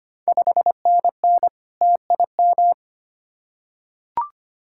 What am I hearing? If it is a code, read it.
Morse code, heard as a keyed tone, about 25 wpm, 700 Hz. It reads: 5NN TIM
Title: Ontario QSO Party Exchange 25wpm